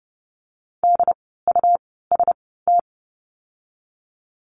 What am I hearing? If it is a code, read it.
Morse code, heard as a keyed tone, about 30 wpm, spaced as for 17 wpm, 700 Hz. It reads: DUST
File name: Lesson 17f - Letter u - Words 30wpm